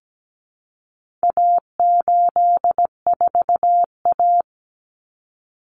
Morse code, heard as a keyed tone, about 17 wpm, 700 Hz. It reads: A84A
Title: Common Cut Numbers - 4-Digit 17wpm